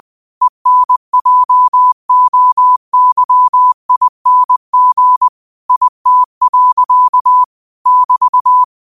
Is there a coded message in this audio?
Yes: Morse code from 0.4 to 8.6 s, heard as a keyed tone, about 20 wpm, 1000 Hz. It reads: ENJOYING IT. <BT>